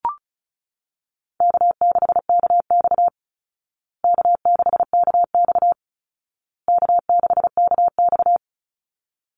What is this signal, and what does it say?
Morse code, heard as a keyed tone, about 34 wpm, 700 Hz. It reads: K6KX K6KX K6KX
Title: Call Signs Repeated 2x - 35wpm